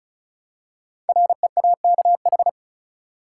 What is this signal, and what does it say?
Morse code, heard as a keyed tone, about 36 wpm, 700 Hz. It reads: REAKH
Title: Code Groups - 5 Letter:Numbers - 35wpm